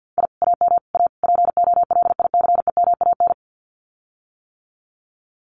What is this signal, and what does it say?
Morse code, heard as a keyed tone, about 50 wpm, 700 Hz. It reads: I AM A POLICEMAN
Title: Common QSO Phrases Encore 50wpm